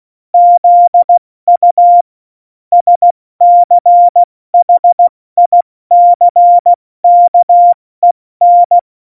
Morse code, heard as a keyed tone, about 16 wpm, 700 Hz. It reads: ZU SCHICKEN
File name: Morsekurs / Witze / Witz 18. Juni 2019, 70 F80